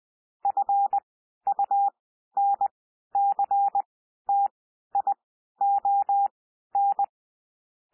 Morse code, heard as a keyed tone, about 20 wpm, spaced as for 12 wpm, 800 Hz. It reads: FUNCTION
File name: cwmp3 / P1 SOUND PASSAGE 20 wpm